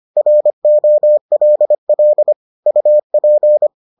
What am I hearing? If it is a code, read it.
Morse code, heard as a keyed tone, about 25 wpm, 600 Hz. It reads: ROLL UP